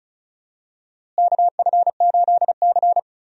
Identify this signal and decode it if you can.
Morse code, heard as a keyed tone, about 36 wpm, 700 Hz. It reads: KF8C
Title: USA Call Signs - 4 Characters - Repeated 2x - 35wpm